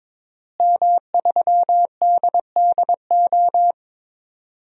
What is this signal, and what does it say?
Morse code, heard as a keyed tone, about 21 wpm, 700 Hz. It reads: M3DDO
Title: DXCC-100 Call Signs - 5 Characters - Repeated 3x - 22wpm